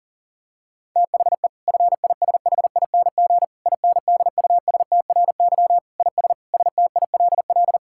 Morse code, heard as a keyed tone, about 40 wpm, 700 Hz. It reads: THE FISHING INDUSTRY IS STILL
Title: Sentences from Top 1000 Words - Review Part 6 - 40wpm